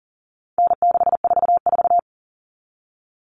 Morse code, heard as a keyed tone, about 40 wpm, 700 Hz. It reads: N644
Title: Common Cut Numbers - 4-Digit 40wpm